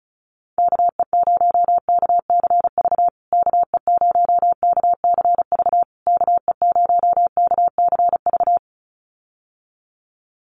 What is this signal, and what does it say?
Morse code, heard as a keyed tone, about 34 wpm, 700 Hz. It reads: KE0KCV KE0KCV KE0KCV
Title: USA Call Signs Repeated 3x 35wpm